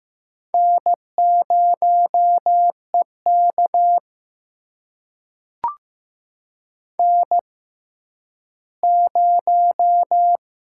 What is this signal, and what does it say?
Morse code, heard as a keyed tone, about 15 wpm, spaced as for 10 wpm, 700 Hz. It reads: N0EK N0
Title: USA Call Signs - 4 Characters - 6x Character Spacing - 15wpm